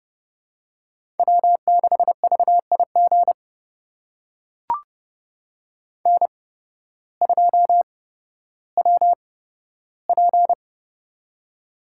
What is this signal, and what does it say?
Morse code, heard as a keyed tone, about 30 wpm, spaced as for 14 wpm, 700 Hz. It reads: W6VIG N2WP
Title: USA Call Signs - 8x Character Spacing - 30wpm